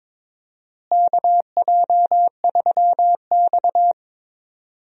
Morse code, heard as a keyed tone, about 22 wpm, 700 Hz. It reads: KJ3X
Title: USA Call Signs - 4 Characters - Repeated 2x - 22wpm